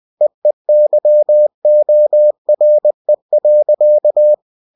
Morse code, heard as a keyed tone, about 20 wpm, 600 Hz. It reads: EEYORE.
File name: cw-books / winnie-the-pooh / winnie-20wpm-page-0080